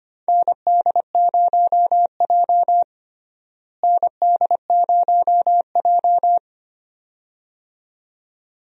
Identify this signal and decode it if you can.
Morse code, heard as a keyed tone, about 25 wpm, 700 Hz. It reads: ND0J ND0J
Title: USA Call Signs - 4 Characters - Repeated 2x - 25wpm